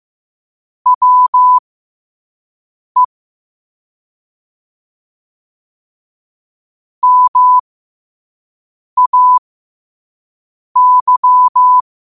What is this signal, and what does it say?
Morse code, heard as a keyed tone, about 15 wpm, spaced as for 5 wpm, 1000 Hz. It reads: WE MAY